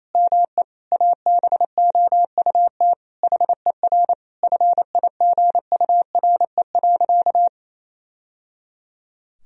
Morse code, heard as a keyed tone, about 28 wpm, 700 Hz. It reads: ME ABOUT HER FIGURE.
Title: Sentences from Top 1000 Words - Review Part 2 - 28wpm